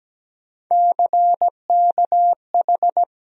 Morse code, heard as a keyed tone, about 17 wpm, 700 Hz. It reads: CKH
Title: USA Call Sign Postfixes - 3 Letters 17wpm